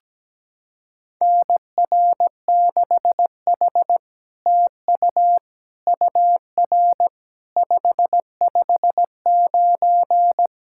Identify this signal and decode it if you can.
Morse code, heard as a keyed tone, about 17 wpm, 700 Hz. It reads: NR6H TU UR 559
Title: SOTA QSO Practice 17wpm